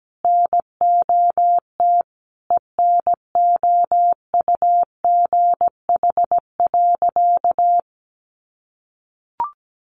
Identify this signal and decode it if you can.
Morse code, heard as a keyed tone, about 17 wpm, 700 Hz. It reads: NOT ENOUGH.
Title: Sentences from Top 600 Words 17wpm